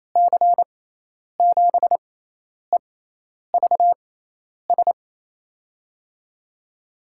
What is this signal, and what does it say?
Morse code, heard as a keyed tone, about 28 wpm, spaced as for 10 wpm, 700 Hz. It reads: C7EVS
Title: USA Call Signs - 6 Characters - 6x Character Spacing - 28wpm